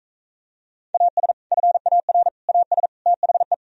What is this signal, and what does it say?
Morse code, heard as a keyed tone, about 44 wpm, 700 Hz. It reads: AS FAR AS THE